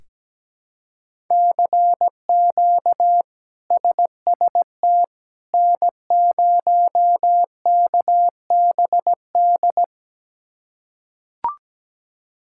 Morse code, heard as a keyed tone, about 17 wpm, 700 Hz. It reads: CQ SST N0KBD
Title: K1USN SST QSO Practice - 17wpm